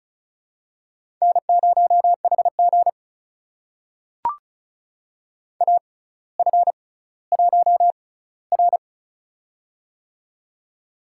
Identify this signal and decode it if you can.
Morse code, heard as a keyed tone, about 36 wpm, spaced as for 18 wpm, 700 Hz. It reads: N0HG AF1R
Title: USA Call Signs - 4 Characters - 6x Character Spacing - 35wpm